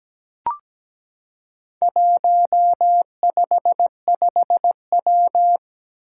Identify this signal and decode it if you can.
Morse code, heard as a keyed tone, about 17 wpm, 700 Hz. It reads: 155W
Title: Four Challenging Letter-Number Combinations 17wpm